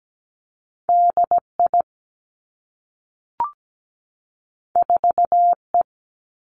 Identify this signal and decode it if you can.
Morse code, heard as a keyed tone, about 17 wpm, 700 Hz. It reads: DI 4E
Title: Two Challenging Letter-Number Combinations 17wpm